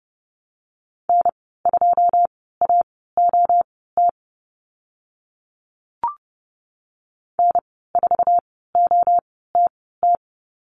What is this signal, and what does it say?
Morse code, heard as a keyed tone, about 30 wpm, spaced as for 17 wpm, 700 Hz. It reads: N2AOT N4OTT
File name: Lesson 14g - Number 2 - Call Signs 30wpm